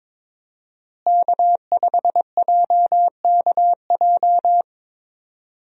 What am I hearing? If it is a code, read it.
Morse code, heard as a keyed tone, about 22 wpm, 700 Hz. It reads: K5JKJ
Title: USA Call Signs - 5 Characters - Repeated 3x - 22wpm